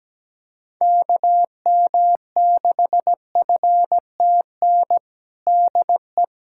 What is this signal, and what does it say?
Morse code, heard as a keyed tone, about 17 wpm, 700 Hz. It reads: KM6FTN DE
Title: USA Callsign DE Callsign 17wpm